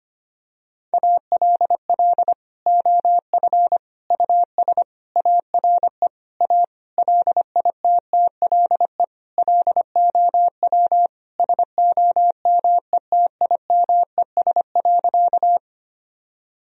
Morse code, heard as a keyed tone, about 25 wpm, 700 Hz. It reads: ALL OF US ARE A LITTLE LOW SOMETIMES.
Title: Sentences from Top 1000 Words - Review Part 3 - 25wpm